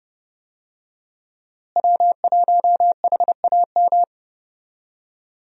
Morse code, heard as a keyed tone, about 29 wpm, 700 Hz. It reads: W1HAM
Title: USA Call Signs - 5 Characters - Repeated 2x - 30wpm